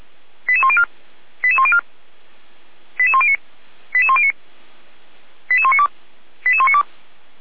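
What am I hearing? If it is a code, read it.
Five-tone selective call (ZVEI), heard as quick runs of five tones, tones of about 70 ms. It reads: 88104 88104 88109 88109 88192 88192
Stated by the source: Five-tone ZVEI selective call 88102 received by SDR, from the Feuerwehr Orlamünde fire brigade audio archive.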